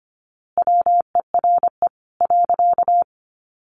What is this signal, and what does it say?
Morse code, heard as a keyed tone, about 25 wpm, 700 Hz. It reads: WERE .
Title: A Sound Of Thunder 25wpm